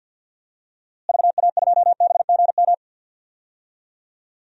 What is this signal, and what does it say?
Morse code, heard as a keyed tone, about 51 wpm, 700 Hz. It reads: VA2BGM
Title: DXCC-100 Call Signs - 6 Characters - Repeated 2x - 50wpm